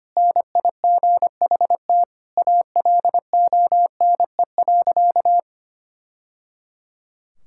Morse code, heard as a keyed tone, about 25 wpm, 700 Hz. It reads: NIGHT ALONE.
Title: Sentences from Top 1000 Words - Review Part 5 - 25wpm